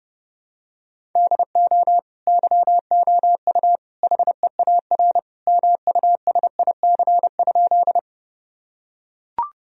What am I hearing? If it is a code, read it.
Morse code, heard as a keyed tone, about 31 wpm, 700 Hz. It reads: DO YOU HEAR MUSIC?
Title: Sentences from Top 1000 Words - Review Part 1 - 30wpm